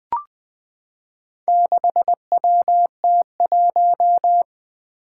Morse code, heard as a keyed tone, about 20 wpm, 700 Hz. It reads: 6WT1
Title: Four Challenging Letter-Number Combinations 20wpm